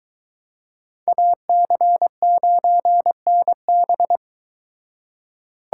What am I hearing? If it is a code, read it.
Morse code, heard as a keyed tone, about 22 wpm, 700 Hz. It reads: AC9NB